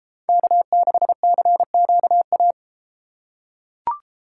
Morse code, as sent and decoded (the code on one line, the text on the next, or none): -.- -.... -.-. --.- .-
K6CQA